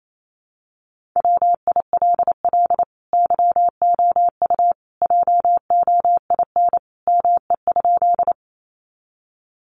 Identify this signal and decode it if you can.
Morse code, heard as a keyed tone, about 28 wpm, 700 Hz. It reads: WILL YOU JOIN ME?
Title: Sentences from Top 1400 Words 28wpm